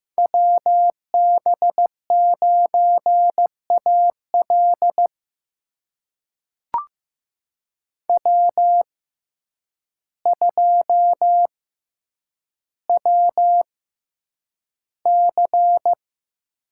Morse code, heard as a keyed tone, about 15 wpm, spaced as for 8 wpm, 700 Hz. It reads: WB9AL W2WC
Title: USA Call Signs - 6x Character Spacing - 15wpm